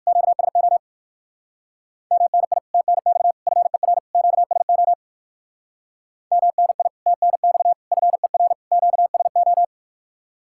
Morse code, heard as a keyed tone, about 63 wpm, 700 Hz. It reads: QSO MNI TNX FER QSO MNI TNX FER QSO